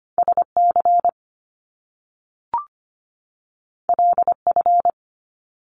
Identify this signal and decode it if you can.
Morse code, heard as a keyed tone, about 25 wpm, 700 Hz. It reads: SC LF
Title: Top Two Letter Combinations in Words 25wpm